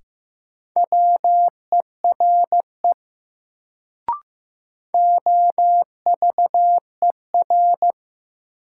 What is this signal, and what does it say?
Morse code, heard as a keyed tone, about 15 wpm, 700 Hz. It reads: WERE OVER
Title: Sentences from Top 100 Words - New Word Drill - 15wpm